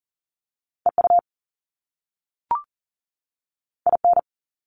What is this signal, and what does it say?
Morse code, heard as a keyed tone, about 41 wpm, 700 Hz. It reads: EU IN
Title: Top Two Letter Combinations in Words 40wpm